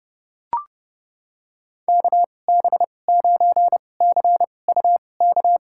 Morse code, heard as a keyed tone, about 30 wpm, spaced as for 22 wpm, 700 Hz. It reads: KB9CUK
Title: USA Call Signs - 6 Characters - 2x Character Spacing - 30wpm